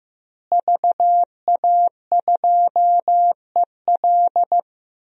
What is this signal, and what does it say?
Morse code, heard as a keyed tone, about 16 wpm, 700 Hz. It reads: VA2EL
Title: DXCC-100 Call Signs - 5 Characters - Repeated 2x - 15wpm